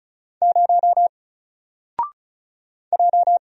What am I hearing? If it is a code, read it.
Morse code, heard as a keyed tone, about 36 wpm, 700 Hz. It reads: 0 J
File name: Single-Letter-Number-35wpm